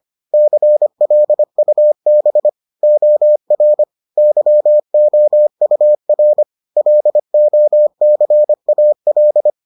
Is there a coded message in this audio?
Yes: Morse code from 0.3 to 9.5 s, heard as a keyed tone, about 25 wpm, 600 Hz. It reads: CLUB OR YOUR LOCAL